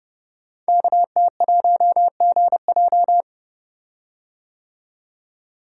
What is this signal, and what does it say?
Morse code, heard as a keyed tone, about 29 wpm, 700 Hz. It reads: KT1GJ
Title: Code Groups - 5 Letter:Numbers - 30wpm